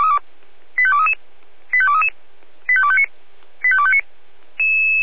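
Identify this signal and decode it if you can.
Five-tone selective call (ZVEI), heard as quick runs of five tones, tones of about 70 ms. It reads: 85233 85233 85259 85259
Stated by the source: ZVEI five-tone selective call received on SDR.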